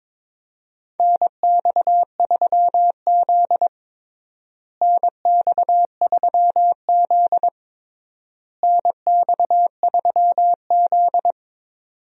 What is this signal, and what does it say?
Morse code, heard as a keyed tone, about 21 wpm, 700 Hz. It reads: NX3Z NX3Z NX3Z
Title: USA Call Signs - 4 Characters - Repeated 3x - 22wpm